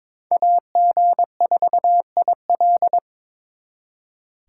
Morse code, heard as a keyed tone, about 26 wpm, 700 Hz. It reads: AG4IL
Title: Speed Racing - USA 2x2 Call Signs 22wpm